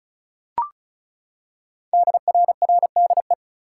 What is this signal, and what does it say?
Morse code, heard as a keyed tone, about 36 wpm, 700 Hz. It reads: DRRDE
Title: Code Groups - 5 Letters - 35wpm